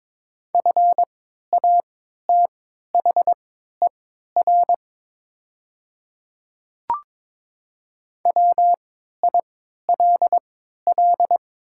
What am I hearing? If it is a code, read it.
Morse code, heard as a keyed tone, about 22 wpm, spaced as for 12 wpm, 700 Hz. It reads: FATHER WILL
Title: 3x Character Spacing - Top 300 Words - 22wpm